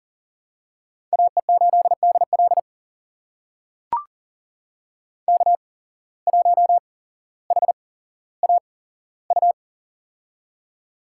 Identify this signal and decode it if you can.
Morse code, heard as a keyed tone, about 40 wpm, spaced as for 17 wpm, 700 Hz. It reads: AE8DL K1HAU
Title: USA Call Signs - 8x Character Spacing - 40wpm